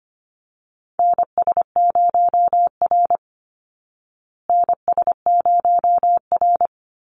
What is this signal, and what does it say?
Morse code, heard as a keyed tone, about 25 wpm, 700 Hz. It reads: NS0R NS0R
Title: Call Signs Repeated 2x - 25wpm